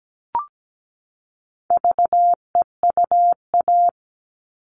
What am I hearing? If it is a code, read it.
Morse code, heard as a keyed tone, about 17 wpm, 700 Hz. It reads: VEUA
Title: All Cut Numbers - 4-Digit 17wpm